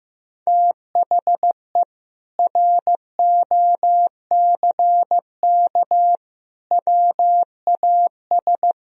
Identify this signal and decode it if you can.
Morse code, heard as a keyed tone, about 15 wpm, 700 Hz. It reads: THE ROCK WAS
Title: Sentences from Top 400 Words and Abbreviations 15wpm